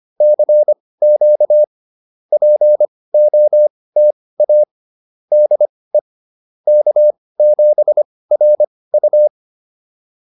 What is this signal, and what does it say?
Morse code, heard as a keyed tone, about 25 wpm, spaced as for 18 wpm, 590 Hz. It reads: CQ POTA DE K7RU
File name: POTA206 18